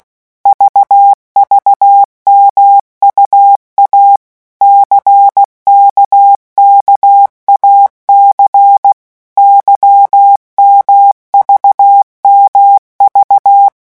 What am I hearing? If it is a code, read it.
Morse code, heard as a keyed tone, about 16 wpm, 800 Hz. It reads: VVMUA CKKAC YMVMV
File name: morse mp3 / revision2-vitesse 15 mots